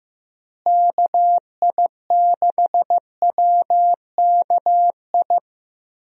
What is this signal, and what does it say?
Morse code, heard as a keyed tone, about 15 wpm, 700 Hz. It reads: KI6WKI
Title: USA Call Signs - 6 Characters - Repeated 3x - 15wpm